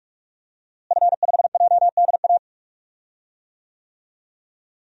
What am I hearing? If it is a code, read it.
Morse code, heard as a keyed tone, about 41 wpm, 700 Hz. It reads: F5JDA
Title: DXCC-100 Call Signs - 5 Characters - Repeated 3x - 45wpm